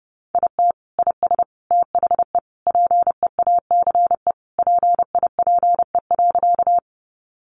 Morse code, heard as a keyed tone, about 30 wpm, 700 Hz. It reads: IT IS THE PEACE PIPE.
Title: Sentences from Top 1500 Words 30wpm